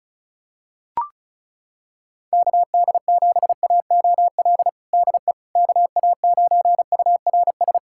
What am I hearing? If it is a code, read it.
Morse code, heard as a keyed tone, about 36 wpm, 690 Hz. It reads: KD7AOL DE KA9URS
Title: USA Callsign DE Callsign 35wpm